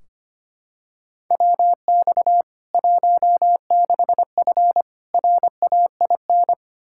Morse code, heard as a keyed tone, about 25 wpm, 700 Hz. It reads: WX 16F RAIN
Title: K1USN SST QSO Practice 25wpm